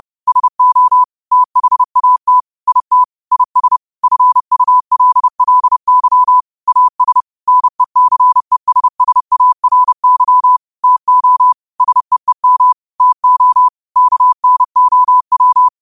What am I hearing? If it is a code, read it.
Morse code, heard as a keyed tone, about 30 wpm, 1000 Hz. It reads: SO THAT IT IS FULLY AS NECESSARY TO SEEM TO KNOW
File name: qotdfiles / 30wpm